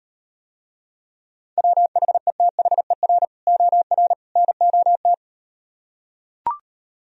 Morse code, heard as a keyed tone, about 38 wpm, 700 Hz. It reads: WHETHER OR NOT